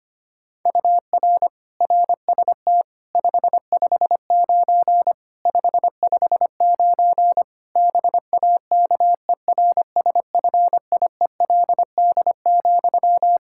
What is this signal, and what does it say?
Morse code, heard as a keyed tone, about 25 wpm, 700 Hz. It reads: UR RST 559 559 BAKERSFIELD,
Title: K1USN SST QSO Practice 25wpm